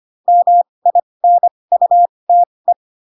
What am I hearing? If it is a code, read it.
Morse code, heard as a keyed tone, about 25 wpm, spaced as for 20 wpm, 700 Hz. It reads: MINUTE
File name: Pinocchio / Pinocchio ch6 15 25 ik1hge